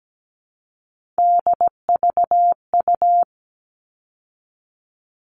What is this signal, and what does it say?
Morse code, heard as a keyed tone, about 17 wpm, 700 Hz. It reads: DVU
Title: All Cut Numbers - 3-Digit 17wpm